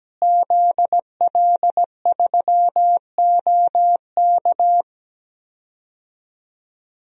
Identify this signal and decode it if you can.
Morse code, heard as a keyed tone, about 17 wpm, 700 Hz. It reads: ZL3OK
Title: DXCC-100 Call Signs - 5 Characters - Repeated 2x - 17wpm